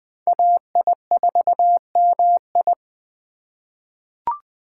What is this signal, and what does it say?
Morse code, heard as a keyed tone, about 20 wpm, 700 Hz. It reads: AI4MI
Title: K1USN SST QSO Practice - 20wpm